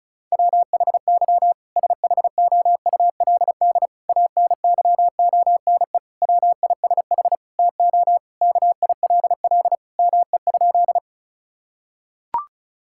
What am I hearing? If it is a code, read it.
Morse code, heard as a keyed tone, about 36 wpm, 700 Hz. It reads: WHY SHOULD ANYONE WISH TO KILL ME?
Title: Sentences from Top 1600 Words 35wpm